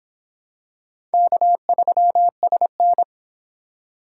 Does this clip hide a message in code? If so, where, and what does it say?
Yes: Morse code from 1.1 to 3.0 s, heard as a keyed tone, about 25 wpm, 700 Hz. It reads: K3SN